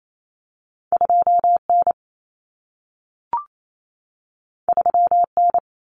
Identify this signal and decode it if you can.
Morse code, heard as a keyed tone, about 28 wpm, 700 Hz. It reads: 2N 3N
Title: Common Cut Numbers - 2-Digit 28wpm